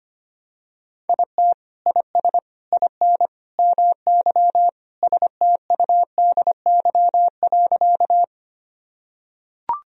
Morse code, heard as a keyed tone, about 25 wpm, 700 Hz. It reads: IT IS IN MY STUDY.
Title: Sentences from Top 1000 Words - Review Part 1 - 25wpm